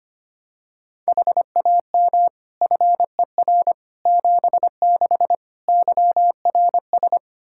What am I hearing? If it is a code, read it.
Morse code, heard as a keyed tone, about 25 wpm, 700 Hz. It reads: HAM FER 76 YRS
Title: Practice QSO - Exchange 3 - More Info - 25wpm